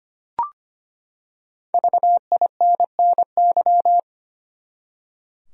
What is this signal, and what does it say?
Morse code, heard as a keyed tone, about 25 wpm, 700 Hz. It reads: VINNY
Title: USA Common Names - 25wpm